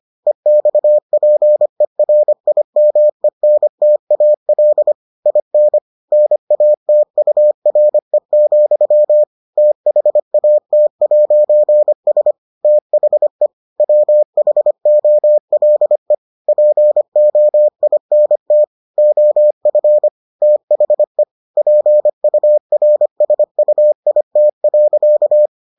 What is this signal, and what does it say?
Morse code, heard as a keyed tone, about 25 wpm, 600 Hz. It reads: EXPERIMENTAL IN NATURE, THAT'S THE WHOLE POINT OF THE PURSUIT.